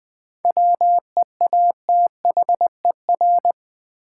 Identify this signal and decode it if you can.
Morse code, heard as a keyed tone, about 20 wpm, 700 Hz. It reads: WEATHER
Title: Sentences from Top 800 Words - New Word Drill - 20wpm